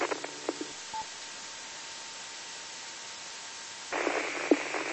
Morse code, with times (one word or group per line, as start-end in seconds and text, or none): none